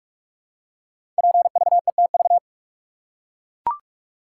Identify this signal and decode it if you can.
Morse code, heard as a keyed tone, about 44 wpm, 690 Hz. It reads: PVETV
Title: Code Groups - 5 Letters - 45wpm